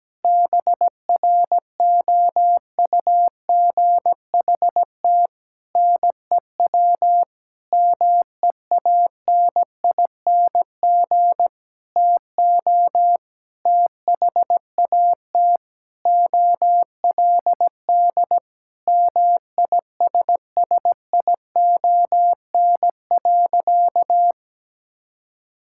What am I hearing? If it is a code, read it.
Morse code, heard as a keyed tone, about 17 wpm, 700 Hz. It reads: BROUGHT NEW MEANING TO THAT OLD MISSION.
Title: Sentences from Top 1600 Words 17wpm